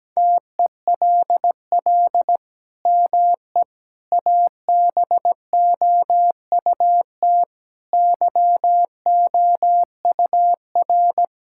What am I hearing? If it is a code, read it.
Morse code, heard as a keyed tone, about 17 wpm, 700 Hz. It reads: TELL ME ABOUT YOUR
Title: Sentences from Top 1000 Words - Review Part 6 - 17wpm